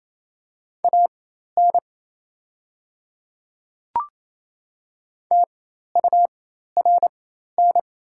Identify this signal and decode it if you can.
Morse code, heard as a keyed tone, about 28 wpm, spaced as for 13 wpm, 700 Hz. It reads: AN TURN
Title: 4x Character Spacing - Top 200 Words - 28wpm